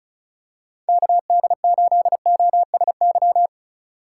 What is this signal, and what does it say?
Morse code, heard as a keyed tone, about 34 wpm, 700 Hz. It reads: KD8OSY
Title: USA Call Signs - 6 Characters - 3x Character Spacing - 35wpm